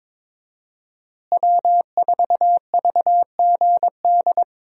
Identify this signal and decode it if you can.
Morse code, heard as a keyed tone, about 22 wpm, 700 Hz. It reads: W4VGD
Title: USA Call Signs - 5 Characters - Repeated 3x - 22wpm